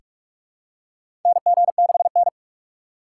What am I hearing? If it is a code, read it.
Morse code, heard as a keyed tone, about 46 wpm, 700 Hz. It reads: NG6N
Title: USA Call Signs - 4 Characters - 4x Character Spacing - 45wpm